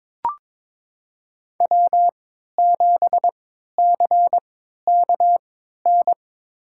Morse code, heard as a keyed tone, about 22 wpm, spaced as for 13 wpm, 700 Hz. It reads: W7CKN
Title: USA Call Signs - 3x Character Spacing - 22wpm